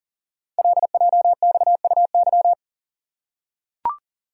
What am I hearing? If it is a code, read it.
Morse code, heard as a keyed tone, about 42 wpm, 690 Hz. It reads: LJXUY
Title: Code Groups - 5 Letter:Numbers - 40wpm